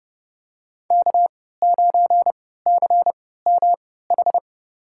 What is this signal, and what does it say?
Morse code, heard as a keyed tone, about 30 wpm, spaced as for 17 wpm, 700 Hz. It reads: K9CMH
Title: USA Call Signs - 5 Characters - 3x Character Spacing - 30wpm